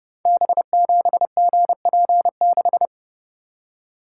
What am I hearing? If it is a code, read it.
Morse code, heard as a keyed tone, about 30 wpm, 700 Hz. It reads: B7GP6
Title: Code Groups - 5 Letter:Numbers - 30wpm